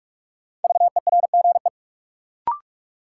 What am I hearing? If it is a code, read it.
Morse code, heard as a keyed tone, about 42 wpm, 700 Hz. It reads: VERGE
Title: Sets of 4 Words - New Word Drill - 45wpm